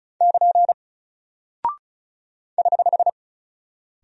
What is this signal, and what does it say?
Morse code, heard as a keyed tone, about 36 wpm, 700 Hz. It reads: <KN> <HH>
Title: Single Character (Letter:Number:Punctuation:Prosign) 35wpm